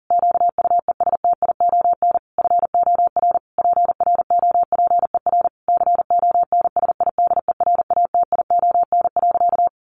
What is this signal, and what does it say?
Morse code, heard as a keyed tone, about 37 wpm, 700 Hz. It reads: QUESTION FOR PROPER CONSIDERATION.
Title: Sentences from Top 1500 Words 40wpm